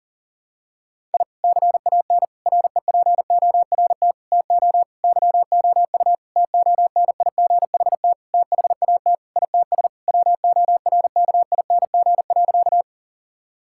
Morse code, heard as a keyed tone, about 40 wpm, 700 Hz. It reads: I CAN REPORT TO YOU TONIGHT THAT ITS WORKING.
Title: Sentences from Top 1000 Words - Review Part 6 - 40wpm